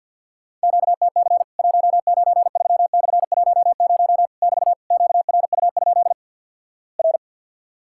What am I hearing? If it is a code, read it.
Morse code, heard as a keyed tone, about 53 wpm, 700 Hz. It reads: QTC 193/10 <BT> QRU? R
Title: Worked All Europe DX Contest - QTC Exchange - 50wpm